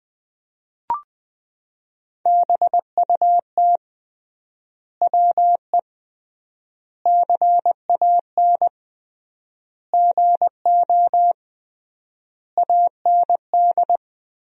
Morse code, heard as a keyed tone, about 20 wpm, 700 Hz. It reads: BUT WE CAN GO AND
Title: 3x Word-Spacing - Sentences from Top 100 Words - 20wpm